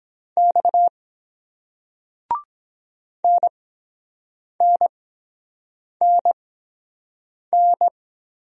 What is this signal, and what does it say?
Morse code, heard as a keyed tone, about 23 wpm, spaced as for 7 wpm, 700 Hz. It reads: X NNNN